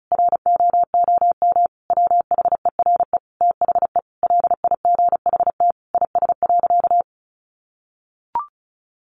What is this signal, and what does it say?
Morse code, heard as a keyed tone, about 34 wpm, 700 Hz. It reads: ROOM WHERE THE LIGHT IS.
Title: Sentences from Top 1000 Words 35wpm